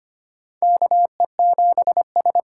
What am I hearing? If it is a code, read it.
Morse code, heard as a keyed tone, about 25 wpm, 700 Hz. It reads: KE7S
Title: USA Call Signs - 6 Characters - 6x Character Spacing - 25wpm